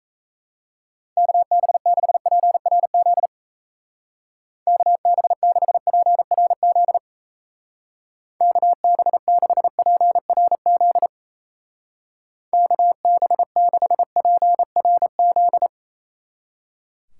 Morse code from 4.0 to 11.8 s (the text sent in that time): KB6PRZ KB6PRZ